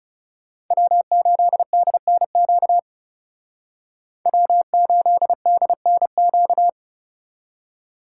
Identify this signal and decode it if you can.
Morse code, heard as a keyed tone, about 36 wpm, 690 Hz. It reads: W8DNQ W8DNQ
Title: Speed Racing - USA Call Signs 30wpm